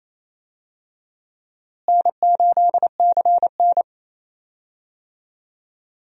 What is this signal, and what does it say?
Morse code, heard as a keyed tone, about 28 wpm, 700 Hz. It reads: N8CN
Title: USA Call Signs - 4 Characters - Repeated 2x - 28wpm